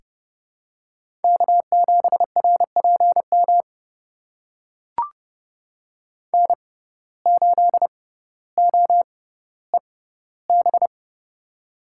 Morse code, heard as a keyed tone, about 30 wpm, spaced as for 15 wpm, 700 Hz. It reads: K7RPM N8OEB
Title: USA Call Signs - 5 Characters - 6x Character Spacing - 30wpm